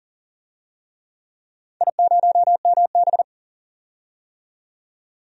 Morse code, heard as a keyed tone, about 40 wpm, 700 Hz. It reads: I0MB
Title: DXCC-100 Call Signs - 4 Characters - 40wpm